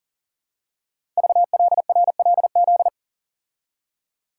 Morse code, heard as a keyed tone, about 40 wpm, 700 Hz. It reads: VLRLZ
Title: Code Groups - 5 Letters - 40wpm